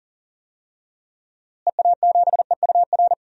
Morse code, heard as a keyed tone, about 38 wpm, 700 Hz. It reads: EA7EUR
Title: DXCC-100 Call Signs - 6 Characters - Repeated 2x - 40wpm